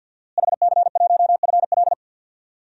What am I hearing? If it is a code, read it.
Morse code, heard as a keyed tone, about 46 wpm, 700 Hz. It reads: HC1FL
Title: DXCC-100 Call Signs - 5 Characters - Repeated 2x - 50wpm